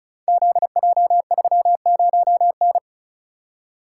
Morse code, heard as a keyed tone, about 36 wpm, 700 Hz. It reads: ZJ30N
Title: Code Groups - 5 Letter:Numbers - 35wpm